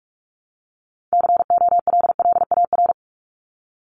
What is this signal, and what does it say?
Morse code, heard as a keyed tone, about 44 wpm, 700 Hz. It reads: COLLAR
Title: Top 1751-2000 Words 45wpm